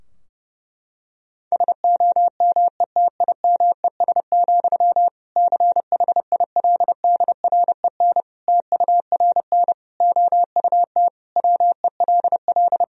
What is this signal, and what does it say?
Morse code, heard as a keyed tone, about 30 wpm, 700 Hz. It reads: SOMETIMES, CHILDREN TURN OUT WELL
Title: Sentences from Top 1000 Words - Review Part 3 - 30wpm